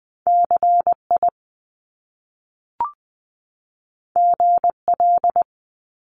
Morse code, heard as a keyed tone, about 20 wpm, 700 Hz. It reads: CI GL
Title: Top Two Letter Combinations in Words 20wpm